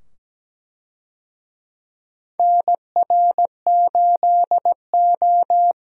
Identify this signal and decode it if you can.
Morse code, heard as a keyed tone, about 17 wpm, 700 Hz. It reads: NR8O
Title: USA Call Signs - 4 Characters - 17wpm